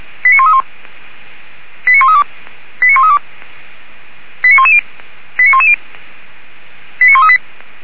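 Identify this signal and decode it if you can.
Five-tone selective call (ZVEI), heard as quick runs of five tones, tones of about 70 ms. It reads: 89131 89132 89132 89110 89110 89138
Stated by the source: Five-tone ZVEI selective call received by SDR, from feuerwehr-orlamuende.org fire brigade audio server.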